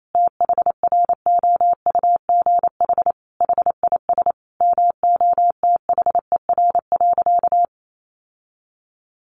Morse code, heard as a keyed tone, about 28 wpm, 700 Hz. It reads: THROUGH HIS MOTHER.
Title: Sentences from Top 1200 Words 28wpm